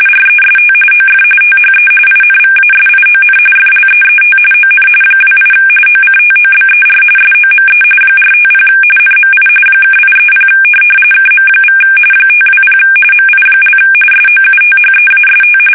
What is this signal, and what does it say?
RTTY, about 45.45 baud. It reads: REST OF THE INVASION FLEET ON 28 OCTOBER SOME 450 NMI (830 KM= 520 MI) SOUTHEAST
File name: fsk850 / USS-Massachusetts